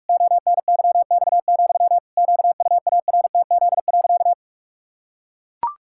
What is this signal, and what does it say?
Morse code, heard as a keyed tone, about 41 wpm, 700 Hz. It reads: ONYX, QUARTZ.
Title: Pangram Challenge v2 - 45wpm